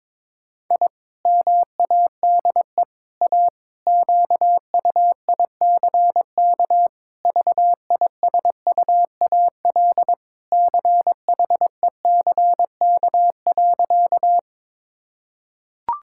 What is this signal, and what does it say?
Morse code, heard as a keyed tone, about 22 wpm, 700 Hz. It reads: I MADE A QUICK VISUAL CHECK.
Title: Sentences from Top 1000 Words - Review Part 6 - 22wpm